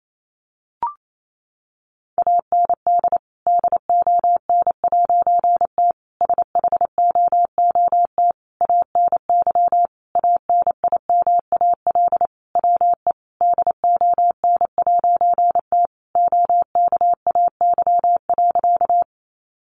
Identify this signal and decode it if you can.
Morse code, heard as a keyed tone, about 28 wpm, 700 Hz. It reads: AND DON'T SHOOT ANY ANIMAL WE DON'T OKAY.
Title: A Sound Of Thunder 28wpm